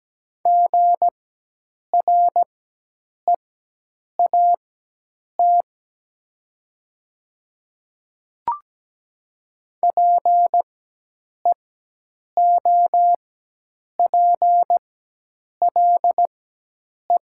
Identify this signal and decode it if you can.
Morse code, heard as a keyed tone, about 17 wpm, spaced as for 8 wpm, 700 Hz. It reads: GREAT PEOPLE
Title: 4x Character Spacing - Top 500 Words - 17wpm